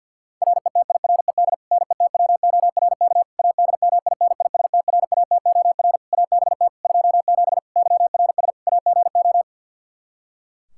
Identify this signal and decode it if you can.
Morse code, heard as a keyed tone, about 52 wpm, 700 Hz. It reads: RETIRED NETWORK ADMINISTRATOR ABT 27 YRS AGO
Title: K1USN SST QSO Practice 50wpm